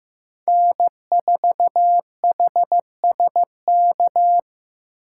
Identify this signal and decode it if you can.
Morse code, heard as a keyed tone, about 18 wpm, 700 Hz. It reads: N4HSK
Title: Speed Racing - USA 1x3 Call Signs 15wpm